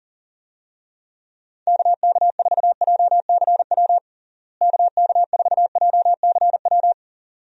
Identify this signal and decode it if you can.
Morse code, heard as a keyed tone, about 37 wpm, 700 Hz. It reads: KK4JCW KK4JCW
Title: USA Call Signs - 6 Characters - Repeated 2x - 40wpm